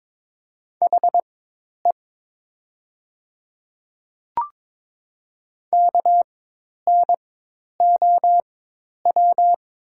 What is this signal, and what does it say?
Morse code, heard as a keyed tone, about 22 wpm, spaced as for 10 wpm, 700 Hz. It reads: HE KNOW
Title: 4x Character Spacing - Top 200 Words - 22wpm